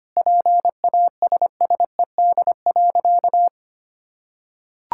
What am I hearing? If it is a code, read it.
Morse code, heard as a keyed tone, about 25 wpm, 700 Hz. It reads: PASSED.
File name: practice / 25wpm / Sentences with 4-Word Phrases - 25wpm